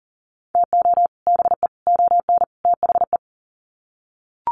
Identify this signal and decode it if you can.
Morse code, heard as a keyed tone, about 40 wpm, 700 Hz. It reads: TO BE ON THE
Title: Sets of 4 Words 40wpm